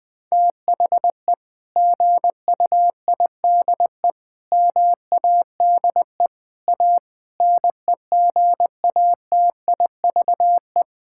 Morse code, heard as a keyed tone, about 21 wpm, 700 Hz. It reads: THE GUIDE MADE A NEGATIVE
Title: Sentences from Top 1000 Words - Review Part 6 - 20wpm